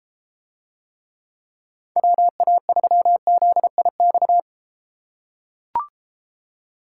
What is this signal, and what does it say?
Morse code, heard as a keyed tone, about 33 wpm, 700 Hz. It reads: WA3ZIX